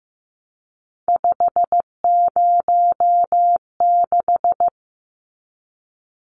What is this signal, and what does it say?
Morse code, heard as a keyed tone, about 15 wpm, 700 Hz. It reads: 506
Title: Numbers - 3-Digit 15wpm